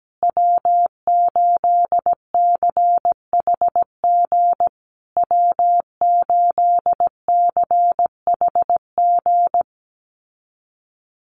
Morse code, heard as a keyed tone, about 17 wpm, 700 Hz. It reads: W8CHG W8CHG
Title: USA Call Signs Repeated 3x 17wpm